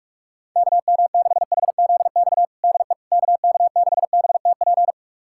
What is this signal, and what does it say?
Morse code, heard as a keyed tone, about 46 wpm, 700 Hz. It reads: KM6HZX DE KK6BTP
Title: USA Callsign DE Callsign 45wpm